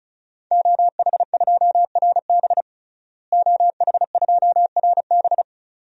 Morse code, heard as a keyed tone, about 36 wpm, 700 Hz. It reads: OH2RB OH2RB
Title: DXCC-100 Call Signs - 5 Characters - Repeated 2x - 35wpm